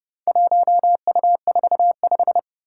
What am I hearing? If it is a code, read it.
Morse code, heard as a keyed tone, about 31 wpm, 700 Hz. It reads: 1U45
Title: Four Challenging Letter-Number Combinations 30wpm